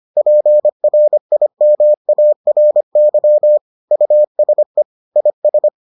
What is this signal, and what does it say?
Morse code, heard as a keyed tone, about 25 wpm, 600 Hz. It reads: PRIMARY USE IS